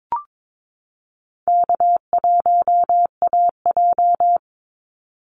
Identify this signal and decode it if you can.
Morse code, heard as a keyed tone, about 22 wpm, 700 Hz. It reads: K1AJ
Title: Call Signs Repeated 3x - 22wpm